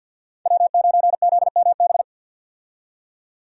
Morse code, heard as a keyed tone, about 52 wpm, 700 Hz. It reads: W9ZMB
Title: USA Call Signs - 5 Characters - 50wpm